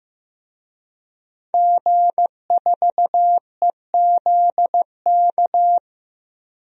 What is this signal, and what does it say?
Morse code, heard as a keyed tone, about 15 wpm, 700 Hz. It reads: G4EZK
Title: DXCC-100 Call Signs - 5 Characters - Repeated 2x - 15wpm